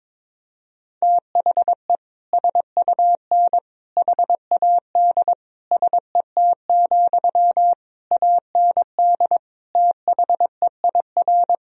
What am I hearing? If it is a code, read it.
Morse code, heard as a keyed tone, about 22 wpm, 700 Hz. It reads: THE SUN HAD SET, AND THEIR
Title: Sentences from Top 1000 Words - Review Part 1 - 22wpm